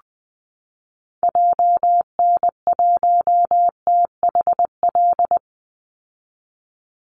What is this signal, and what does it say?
Morse code, heard as a keyed tone, about 20 wpm, 700 Hz. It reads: JN1THL
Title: Call Signs - 20wpm